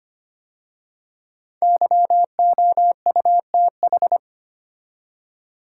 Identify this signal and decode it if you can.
Morse code, heard as a keyed tone, about 25 wpm, 700 Hz. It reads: YOUTH
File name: 5 Letter Words - First 3 Letters Given - 25wpm